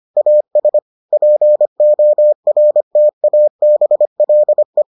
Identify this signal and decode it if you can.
Morse code, heard as a keyed tone, about 25 wpm, 600 Hz. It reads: AS PORTABLE